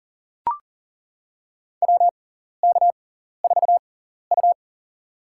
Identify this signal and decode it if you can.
Morse code, heard as a keyed tone, about 40 wpm, spaced as for 14 wpm, 700 Hz. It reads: WK4U
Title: USA Call Signs - 6x Character Spacing - 40wpm